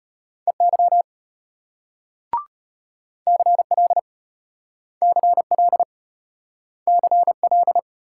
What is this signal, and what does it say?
Morse code, heard as a keyed tone, about 34 wpm, 700 Hz. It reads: EY CL CL CL